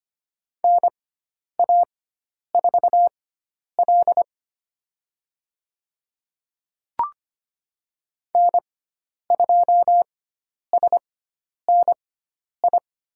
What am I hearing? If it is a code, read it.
Morse code, heard as a keyed tone, about 25 wpm, spaced as for 10 wpm, 700 Hz. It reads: NA4L N2SNI
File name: Lesson 16g - Letter c - Call Signs 25wpm